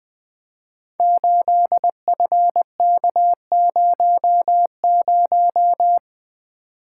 Morse code, heard as a keyed tone, about 20 wpm, 700 Hz. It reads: 8FK00
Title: Code Groups - 5 Letter:Numbers - 20wpm